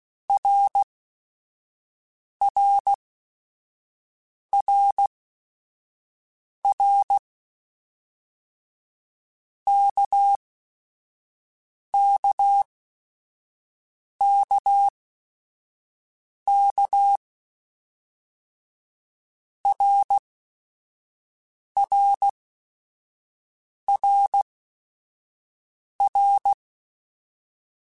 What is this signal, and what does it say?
Morse code, heard as a keyed tone, about 16 wpm, spaced as for 5 wpm, 800 Hz. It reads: RRRR KKKK RRRR
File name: Basic Sets / Basic Letter Groups Collective7 / RKPX@5